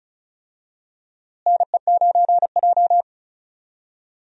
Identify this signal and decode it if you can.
Morse code, heard as a keyed tone, about 36 wpm, 700 Hz. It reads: NE9J
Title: USA Call Signs - 4 Characters - Repeated 3x - 35wpm